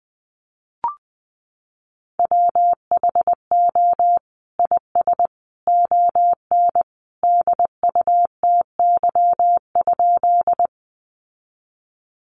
Morse code, heard as a keyed tone, about 20 wpm, 700 Hz. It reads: WHO IS ON DUTY?
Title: Sentences from Top 1500 Words 20wpm